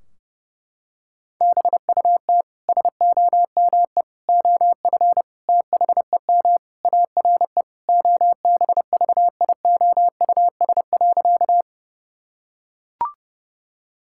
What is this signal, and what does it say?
Morse code, heard as a keyed tone, about 30 wpm, 700 Hz. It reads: BUT SOME OF THEM ARE OBVIOUS.
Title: Sentences from Top 1600 Words 30wpm